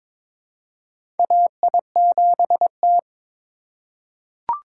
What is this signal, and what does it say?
Morse code, heard as a keyed tone, about 23 wpm, 700 Hz. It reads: AI7T
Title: USA Call Signs - 4 Characters - 4x Character Spacing - 22wpm